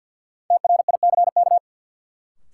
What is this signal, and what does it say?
Morse code, heard as a keyed tone, about 53 wpm, 700 Hz. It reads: TRICK
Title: Sentences from Top 1500 Words - New Word Drill Alt - 50wpm